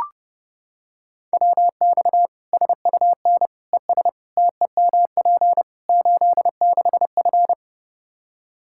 Morse code, heard as a keyed tone, about 30 wpm, 700 Hz. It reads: WX SUN ES TEMP 86F
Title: K1USN SST QSO Practice 30wpm